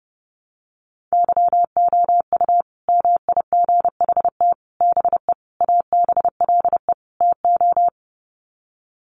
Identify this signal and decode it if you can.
Morse code, heard as a keyed tone, about 30 wpm, 700 Hz. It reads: YOU MIGHT BE ABLE TO
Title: Sets of 5 Words Encore2 30wpm